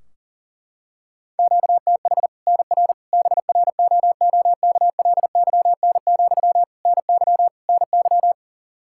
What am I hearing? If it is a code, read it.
Morse code, heard as a keyed tone, about 40 wpm, 700 Hz. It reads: QTH NR BROOKLYN, NY NY
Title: Practice QSO - Exchange 1 - Beginning - 40wpm